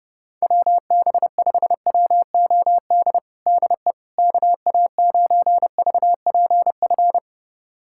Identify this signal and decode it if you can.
Morse code, heard as a keyed tone, about 30 wpm, 700 Hz. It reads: WB5WOD DE KA9VPF
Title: USA Callsign DE Callsign - 6 Characters - 30wpm